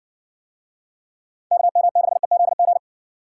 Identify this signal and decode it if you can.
Morse code, heard as a keyed tone, about 57 wpm, 700 Hz. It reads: KM6EZG